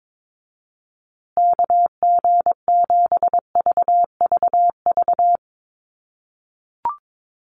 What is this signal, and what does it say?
Morse code, heard as a keyed tone, about 22 wpm, 690 Hz. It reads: KG7VVV
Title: USA Call Signs Repeated 3x 22wpm